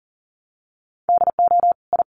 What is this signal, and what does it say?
Morse code, heard as a keyed tone, about 40 wpm, 700 Hz. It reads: DO I
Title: Sentences from Top 1200 Words 40wpm